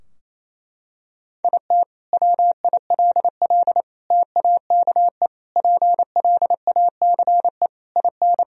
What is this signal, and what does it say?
Morse code, heard as a keyed tone, about 28 wpm, 700 Hz. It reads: IT WILL TAKE PLACE IN
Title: Sentences from Top 1000 Words - Review Part 2 - 28wpm